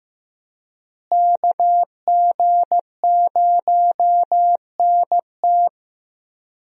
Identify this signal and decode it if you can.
Morse code, heard as a keyed tone, about 16 wpm, 700 Hz. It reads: KG0NT
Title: USA Call Signs - 5 Characters - Repeated 2x - 15wpm